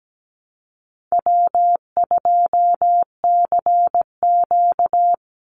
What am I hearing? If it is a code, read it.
Morse code, heard as a keyed tone, about 17 wpm, 700 Hz. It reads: W2CQ
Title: Easy Call Signs Repeated 2x - 17wpm